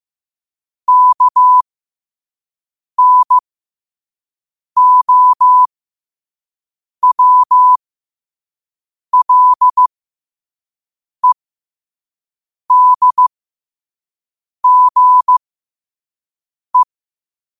Morse code, heard as a keyed tone, about 15 wpm, spaced as for 5 wpm, 1000 Hz. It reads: KNOWLEDGE
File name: qotdfiles / 5wpm